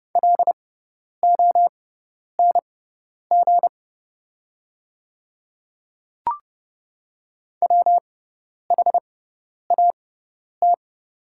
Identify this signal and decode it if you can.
Morse code, heard as a keyed tone, about 31 wpm, spaced as for 10 wpm, 690 Hz. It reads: LONG WHAT
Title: 6x Character Spacing - Top 100 Words - 30wpm